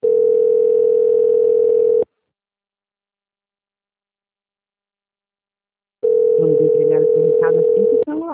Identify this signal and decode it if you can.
Ringback tone (North America), 440 + 480 Hz, with speech.
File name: out- -2005- - - .1082